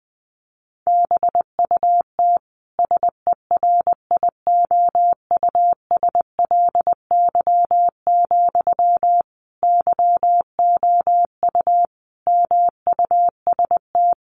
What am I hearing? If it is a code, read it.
Morse code, heard as a keyed tone, about 20 wpm, 700 Hz. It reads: BUT SERIOUSLY, YOU MUST
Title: Sentences from Top 1200 Words 20wpm